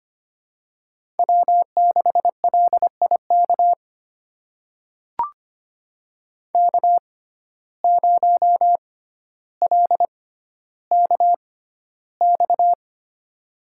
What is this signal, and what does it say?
Morse code, heard as a keyed tone, about 25 wpm, spaced as for 13 wpm, 700 Hz. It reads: W6LIK K0LKX
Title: USA Call Signs - 5 Characters - 6x Character Spacing - 25wpm